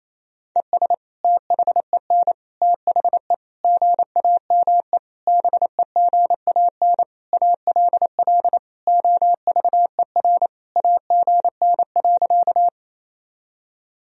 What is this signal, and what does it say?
Morse code, heard as a keyed tone, about 28 wpm, 700 Hz. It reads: ES THEN THE GAME BEGAN ALL OVER AGN.
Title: Sentences from Top 500 Words and Abbreviations 28wpm